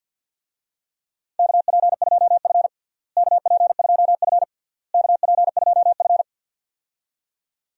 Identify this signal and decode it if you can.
Morse code, heard as a keyed tone, about 48 wpm, 700 Hz. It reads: KP2F KP2F KP2F
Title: DXCC-100 Call Signs - 4 Characters - Repeated 3x - 50wpm